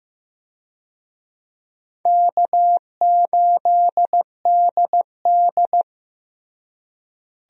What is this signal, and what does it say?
Morse code, heard as a keyed tone, about 15 wpm, 700 Hz. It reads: K8DD
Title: USA Call Signs - 4 Characters - Repeated 3x - 15wpm